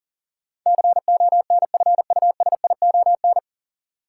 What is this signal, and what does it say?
Morse code, heard as a keyed tone, about 40 wpm, 700 Hz. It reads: CONFUSION
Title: Sentences from Top 1600 Words - New Word Drill - 40wpm